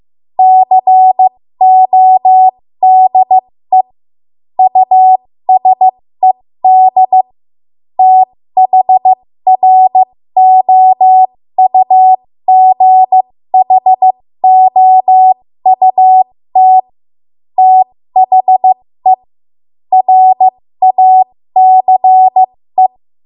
Morse code, heard as a keyed tone, about 15 wpm, spaced as for 13 wpm, 750 Hz. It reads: CODE USED THROUGHOUT THE RACE
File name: Morse / 13WPM